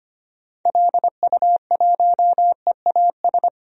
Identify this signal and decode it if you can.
Morse code, heard as a keyed tone, about 25 wpm, 700 Hz. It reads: LU1EAS
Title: DXCC-100 Call Signs - 6 Characters - 25wpm